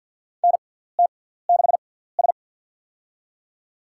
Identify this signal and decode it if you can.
Morse code, heard as a keyed tone, about 52 wpm, spaced as for 17 wpm, 700 Hz. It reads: NT6S
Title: USA Call Signs - 4 Characters - 6x Character Spacing - 50wpm